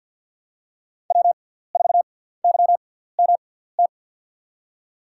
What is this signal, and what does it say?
Morse code, heard as a keyed tone, about 48 wpm, spaced as for 17 wpm, 700 Hz. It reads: W4YMT
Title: USA Call Signs - 5 Characters - 6x Character Spacing - 50wpm